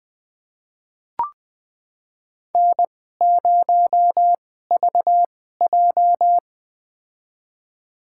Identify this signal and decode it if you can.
Morse code, heard as a keyed tone, about 20 wpm, spaced as for 15 wpm, 700 Hz. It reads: N0VJ
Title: USA Call Signs - 2x Character Spacing - 20wpm